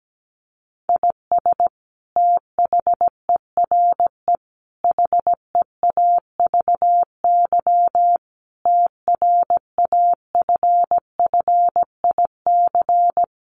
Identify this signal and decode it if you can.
Morse code, heard as a keyed tone, about 17 wpm, 700 Hz. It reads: IS THERE HEAVY TRAFFIC
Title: Sentences from Top 1300 Words 17wpm